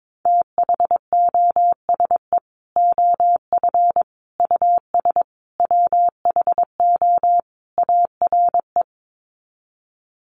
Morse code, heard as a keyed tone, about 22 wpm, 700 Hz. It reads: THOSE OF US WHO ARE
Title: Sets of 5 Words Encore2 22wpm